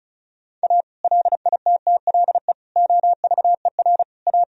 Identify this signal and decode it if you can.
Morse code, heard as a keyed tone, about 36 wpm, 700 Hz. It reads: A LITTLE OVER A
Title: Sentences from Top 1000 Words - Review Part 1 - 35wpm